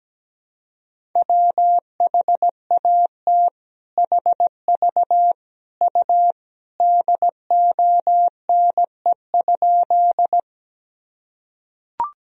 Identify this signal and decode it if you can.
Morse code, heard as a keyed tone, about 17 wpm, 700 Hz. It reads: WHAT HV U DONE?
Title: Sentences from Top 500 Words and Abbreviations 17wpm